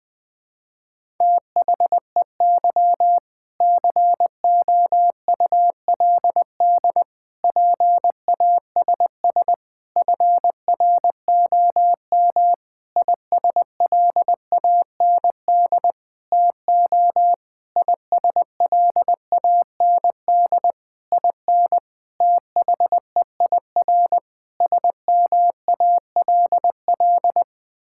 Morse code, heard as a keyed tone, about 20 wpm, 700 Hz. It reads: THEY COULD PASS FROM ISLAND TO ISLAND IN THEIR SMALL
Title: Sentences from Top 1000 Words - Review Part 6 - 20wpm